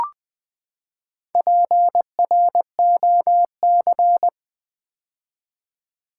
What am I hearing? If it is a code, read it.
Morse code, heard as a keyed tone, about 20 wpm, 700 Hz. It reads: PROC
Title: Build A Word - 4 Letters At A Time - 10-Letter Words - 20wpm